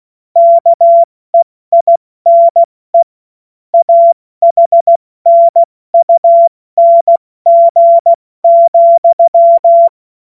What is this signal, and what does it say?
Morse code, heard as a keyed tone, about 16 wpm, 680 Hz. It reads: KEINE AHNUNG,
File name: Morsekurs / Witze / Witz 16. Dezember 2025 70 F80